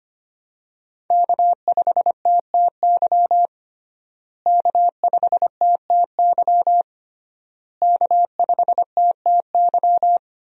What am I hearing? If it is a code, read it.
Morse code, heard as a keyed tone, about 25 wpm, 700 Hz. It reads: K5TTY K5TTY K5TTY
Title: USA Call Signs - 5 Characters - Repeated 3x - 25wpm